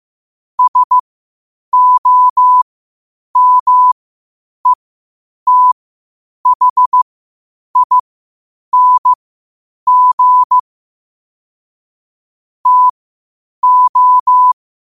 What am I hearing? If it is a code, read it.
Morse code, heard as a keyed tone, about 15 wpm, spaced as for 8 wpm, 1000 Hz. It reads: SOMETHING TO
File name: qotdfiles / 8wpm